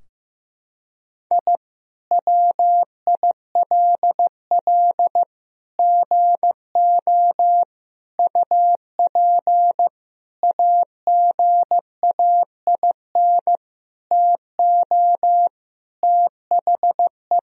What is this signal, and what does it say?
Morse code, heard as a keyed tone, about 15 wpm, 700 Hz. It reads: I WILL GO UP AGAIN TO THE
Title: Sentences from Top 1000 Words - Review Part 3 - 15wpm